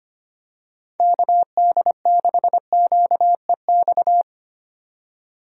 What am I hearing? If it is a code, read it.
Morse code, heard as a keyed tone, about 25 wpm, 700 Hz. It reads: KD6QEX
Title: USA Call Signs - 6 Characters - 25wpm